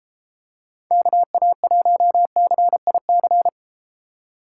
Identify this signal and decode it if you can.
Morse code, heard as a keyed tone, about 31 wpm, 700 Hz. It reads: KA1CIC